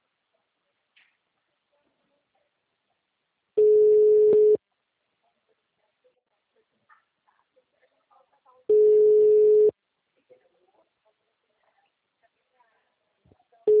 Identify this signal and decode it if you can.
Ringback tone (Europe, 425 Hz), with speech.
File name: out- -2006- - - .92259